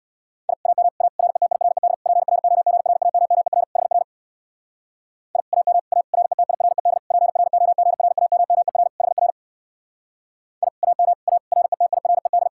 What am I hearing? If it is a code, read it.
Morse code, heard as a keyed tone, about 64 wpm, 700 Hz. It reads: I AM A RETIRED PROGRAMMER HR I AM A RETIRED PROGRAMMER HR I AM A RETIRED